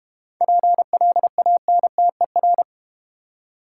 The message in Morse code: PLANTER

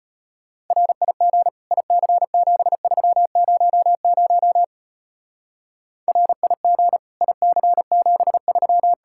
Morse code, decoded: RIG IC7300 RIG IC73

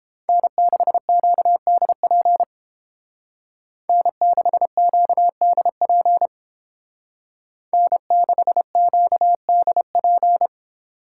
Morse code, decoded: N6QDP N6QDP N6QDP

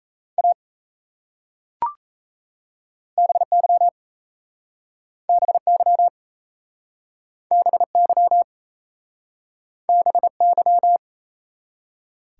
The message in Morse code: A BY BY BY BY

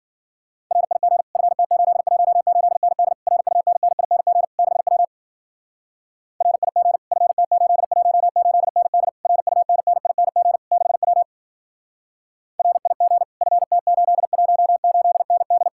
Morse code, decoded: RIG FT818ND RUNNING 6W RIG FT818ND RUNNING 6W RIG FT818ND

